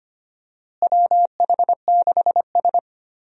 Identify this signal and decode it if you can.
Morse code, heard as a keyed tone, about 25 wpm, 700 Hz. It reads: WH6S
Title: USA Call Signs - 4 Characters - Repeated 2x - 25wpm